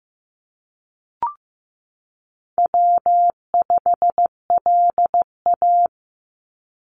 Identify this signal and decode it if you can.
Morse code, heard as a keyed tone, about 15 wpm, 700 Hz. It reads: W5LA
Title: Call Signs Repeated 2x - 15wpm